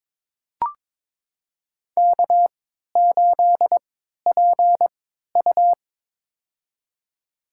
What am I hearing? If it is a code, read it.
Morse code, heard as a keyed tone, about 21 wpm, spaced as for 12 wpm, 700 Hz. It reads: K8PU
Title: USA Call Signs - 4 Characters - 3x Character Spacing - 22wpm